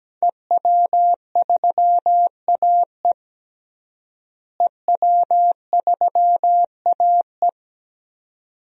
Morse code, heard as a keyed tone, about 17 wpm, 700 Hz. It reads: EW3AE EW3AE
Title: DXCC-100 Call Signs - 5 Characters - Repeated 2x - 17wpm